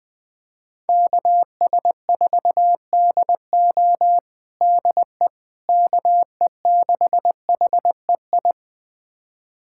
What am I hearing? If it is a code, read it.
Morse code, heard as a keyed tone, about 21 wpm, 700 Hz. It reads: KS4DO DE KE6HEI
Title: USA Callsign DE Callsign 20wpm